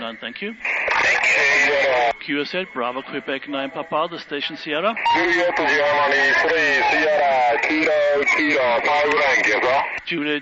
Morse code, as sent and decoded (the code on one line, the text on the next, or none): none